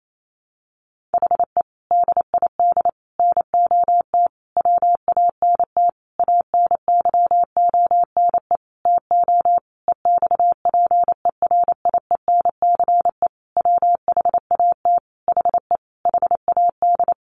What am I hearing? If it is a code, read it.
Morse code, heard as a keyed tone, about 28 wpm, 700 Hz. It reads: HE DID NOT WANT ANYONE TO EXPERIENCE WHAT HE HAD
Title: Sentences from Top 1100 Words 28wpm